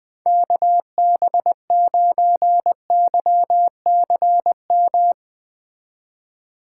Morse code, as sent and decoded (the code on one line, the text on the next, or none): -.- -... ----. -.-- -.-. --
KB9YCM